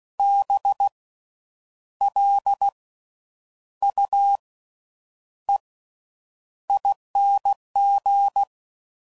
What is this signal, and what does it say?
Morse code, heard as a keyed tone, about 16 wpm, spaced as for 8 wpm, 800 Hz. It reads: BLUEING
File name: Slow With Faster Ings